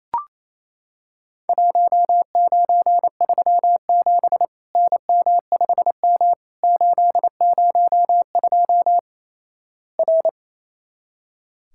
Morse code, heard as a keyed tone, about 28 wpm, 700 Hz. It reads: 1937 NM5M 802 R
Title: Worked All Europe DX Contest - QTC Exchange - 28wpm